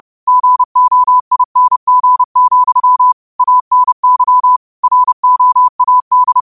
Morse code, heard as a keyed tone, about 30 wpm, 1000 Hz. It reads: GOING, ANY ROAD